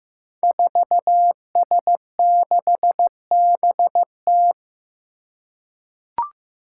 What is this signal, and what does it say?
Morse code, heard as a keyed tone, about 15 wpm, 700 Hz. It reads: 4S6BT
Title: Five Challenging Letter-Number Combinations 15wpm